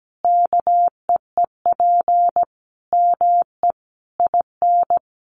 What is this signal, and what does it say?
Morse code, heard as a keyed tone, about 17 wpm, 700 Hz. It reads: KEEP ME IN
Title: Sentences from Top 500 Words 17wpm